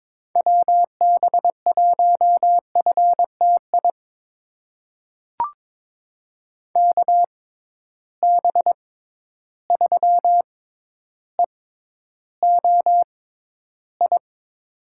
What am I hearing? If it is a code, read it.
Morse code, heard as a keyed tone, about 22 wpm, spaced as for 11 wpm, 700 Hz. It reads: WB1FTI KB3EOI
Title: USA Call Signs - 6 Characters - 6x Character Spacing - 22wpm